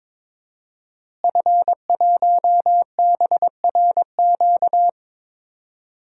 Morse code, heard as a keyed tone, about 22 wpm, 700 Hz. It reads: F1BRQ
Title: DXCC-100 Call Signs - 5 Characters - Repeated 3x - 22wpm